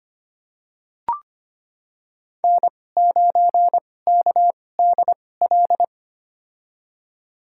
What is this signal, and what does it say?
Morse code, heard as a keyed tone, about 25 wpm, spaced as for 18 wpm, 700 Hz. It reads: N9KDL
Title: USA Call Signs - 2x Character Spacing - 25wpm